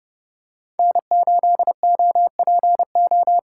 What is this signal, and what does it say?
Morse code, heard as a keyed tone, about 30 wpm, 700 Hz. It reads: N8OPO